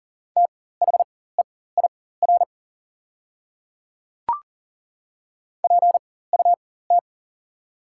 Morse code, heard as a keyed tone, about 39 wpm, spaced as for 19 wpm, 700 Hz. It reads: THEIR PUT
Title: 4x Character Spacing - Top 200 Words - 40wpm